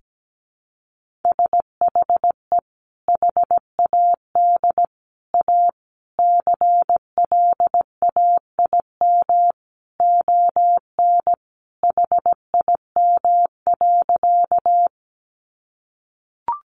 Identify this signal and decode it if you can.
Morse code, heard as a keyed tone, about 17 wpm, 700 Hz. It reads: SHE HAD A CLAIM ON HIM.
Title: Sentences from Top 1500 Words 17wpm